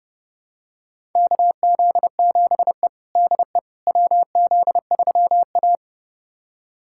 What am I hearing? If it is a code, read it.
Morse code, heard as a keyed tone, about 30 wpm, 700 Hz. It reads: KZ7E DE WZ3A
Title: USA Callsign DE Callsign - 4 Characters - 30wpm